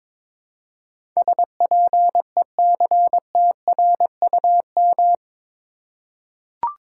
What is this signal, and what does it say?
Morse code, heard as a keyed tone, about 22 wpm, 700 Hz. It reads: SPECTRUM
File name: practice / 22wpm / Eight Letter Words - 22wpm